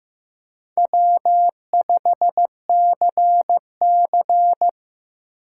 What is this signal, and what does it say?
Morse code, heard as a keyed tone, about 15 wpm, 700 Hz. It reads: W5CC
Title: USA Call Signs - 6x Character Spacing - 15wpm